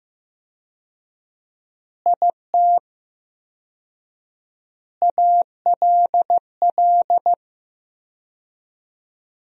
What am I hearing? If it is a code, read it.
Morse code, heard as a keyed tone, about 15 wpm, 700 Hz. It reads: IT ALL
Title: 4x Word-Spacing - Sentences from Top 200 Words - 15wpm